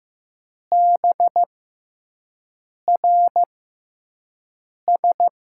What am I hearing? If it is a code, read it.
Morse code, heard as a keyed tone, about 15 wpm, spaced as for 5 wpm, 700 Hz. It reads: BRS
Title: USA Call Signs - 6x Character Spacing - 15wpm